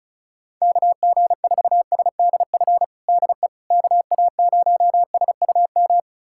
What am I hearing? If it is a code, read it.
Morse code, heard as a keyed tone, about 36 wpm, 700 Hz. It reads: KG4SDF DE KA0SUM
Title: USA Callsign DE Callsign - 6 Characters - 35wpm